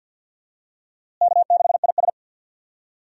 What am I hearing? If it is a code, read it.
Morse code, heard as a keyed tone, about 49 wpm, 700 Hz. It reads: K6IS
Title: USA Call Signs - 4 Characters - Repeated 3x - 50wpm